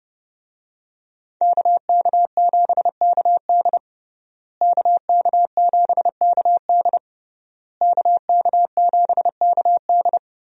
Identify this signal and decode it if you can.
Morse code, heard as a keyed tone, about 30 wpm, 700 Hz. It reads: KK7KD KK7KD KK7KD
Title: USA Call Signs - 5 Characters - Repeated 3x - 30wpm